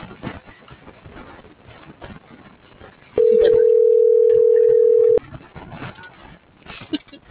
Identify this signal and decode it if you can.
Ringback tone (North America), 440 + 480 Hz.